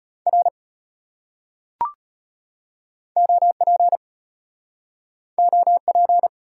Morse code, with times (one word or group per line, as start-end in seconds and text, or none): 0.3-0.5 R
3.2-4.0 OP
5.4-6.3 OP